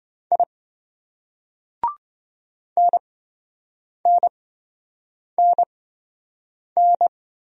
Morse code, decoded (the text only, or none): I NNNN